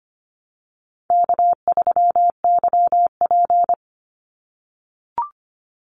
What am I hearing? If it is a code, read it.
Morse code, heard as a keyed tone, about 25 wpm, 700 Hz. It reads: K3YP
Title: Easy Call Signs - 25wpm